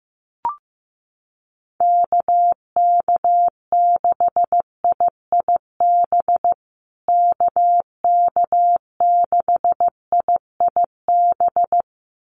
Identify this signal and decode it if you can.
Morse code, heard as a keyed tone, about 15 wpm, 700 Hz. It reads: KK6IIB KK6IIB
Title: USA Call Signs Repeated 2x 15wpm